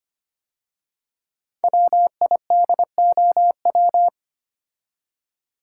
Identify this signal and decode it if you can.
Morse code, heard as a keyed tone, about 25 wpm, 700 Hz. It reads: WIDOW
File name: 5 Letter Words - First 3 Letters Given - 25wpm